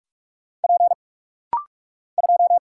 Morse code, heard as a keyed tone, about 44 wpm, 700 Hz. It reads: P 2
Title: Single Letter-Number - Rapid-Fire 45wpm